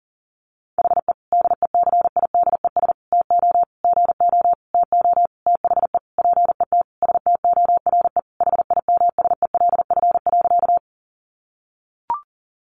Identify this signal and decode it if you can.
Morse code, heard as a keyed tone, about 40 wpm, 700 Hz. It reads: HE DECIDES TO GO TO THE PET STORE HIMSELF.
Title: Easy Phrases 40wpm